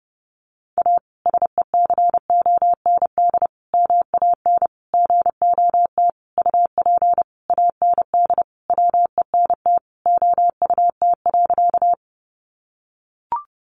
Morse code, heard as a keyed tone, about 30 wpm, 700 Hz. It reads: A SECOND MAN GOT UP AND WENT OUT.
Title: Sentences from Top 300 Words 30wpm